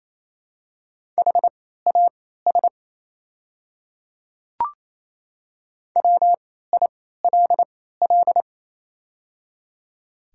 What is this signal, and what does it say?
Morse code, heard as a keyed tone, about 27 wpm, spaced as for 16 wpm, 690 Hz. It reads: HAS WILL
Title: 3x Character Spacing - Top 100 Words - 28wpm